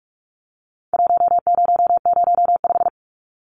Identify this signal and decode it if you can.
Morse code, heard as a keyed tone, about 46 wpm, 700 Hz. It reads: 1005
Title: Numbers - Contest Style - 4-Digit 45wpm